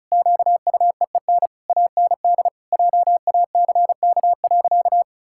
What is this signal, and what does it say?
Morse code, heard as a keyed tone, about 36 wpm, 700 Hz. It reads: QUEEN AND JACK.
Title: Pangram Challenge v2 - 35wpm